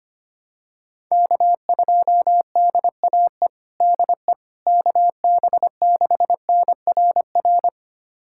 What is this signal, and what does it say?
Morse code, heard as a keyed tone, about 25 wpm, 700 Hz. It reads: K2DAE DE KB6NRR
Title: USA Callsign DE Callsign 25wpm